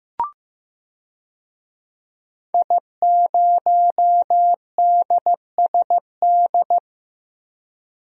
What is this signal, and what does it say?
Morse code, heard as a keyed tone, about 15 wpm, 700 Hz. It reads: I0DSD
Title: DXCC-100 Call Signs - 5 Characters - 15wpm